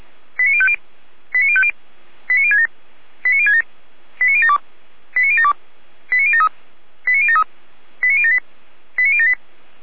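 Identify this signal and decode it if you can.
Five-tone selective call (ZVEI), heard as quick runs of five tones, tones of about 70 ms. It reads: 89050 89050 89067 89067 89072 89072 89073 89073 89078 89078
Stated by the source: Five-tone ZVEI selective call 89010, received by SDR, fire brigade Orlamünde (feuerwehr-orlamuende).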